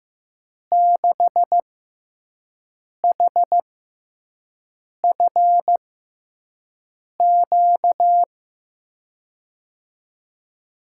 Morse code, heard as a keyed tone, about 15 wpm, spaced as for 5 wpm, 700 Hz. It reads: 6HFQ
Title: USA Call Signs - 6 Characters - 6x Character Spacing - 15wpm